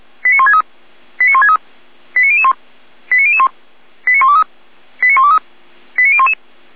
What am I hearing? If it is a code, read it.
Five-tone selective call (ZVEI), heard as quick runs of five tones, tones of about 70 ms. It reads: 89163 89163 89001 89001 89123 89123 89011